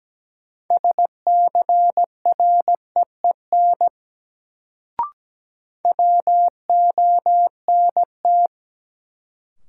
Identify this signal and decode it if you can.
Morse code, heard as a keyed tone, about 17 wpm, 700 Hz. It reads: SCREEN WONT
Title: Sentences from Top 1000 Words - Review Part 6 - New Word Drill - 17wpm